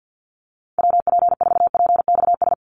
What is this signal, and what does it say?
Morse code, heard as a keyed tone, about 50 wpm, 700 Hz. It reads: WP4PXS
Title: USA Call Signs 50wpm